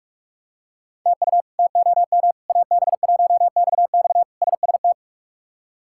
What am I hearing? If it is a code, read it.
Morse code, heard as a keyed tone, about 42 wpm, 700 Hz. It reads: TU TOM AB1XX SST
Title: K1USN SST QSO Practice - 45wpm